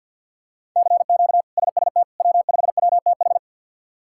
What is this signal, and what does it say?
Morse code, heard as a keyed tone, about 52 wpm, 700 Hz. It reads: CQ SST W5WTH
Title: K1USN SST QSO Practice - 50wpm